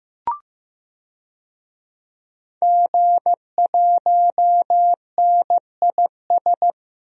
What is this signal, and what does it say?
Morse code, heard as a keyed tone, about 15 wpm, 700 Hz. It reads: G1NIS
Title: DXCC-100 Call Signs - 5 Characters - Repeated 3x - 15wpm